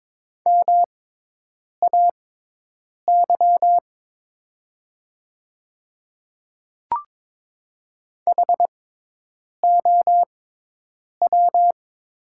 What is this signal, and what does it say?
Morse code, heard as a keyed tone, about 22 wpm, spaced as for 8 wpm, 700 Hz. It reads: MAY HOW
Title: 6x Character Spacing - Top 100 Words - 22wpm